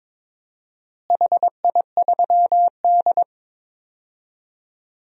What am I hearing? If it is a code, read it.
Morse code, heard as a keyed tone, about 22 wpm, 700 Hz. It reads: HI3D
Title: DXCC-100 Call Signs - 4 Characters - Repeated 3x - 22wpm